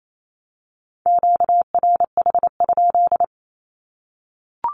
Morse code, heard as a keyed tone, about 28 wpm, 700 Hz. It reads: QRH?
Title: Q-Codes 28wpm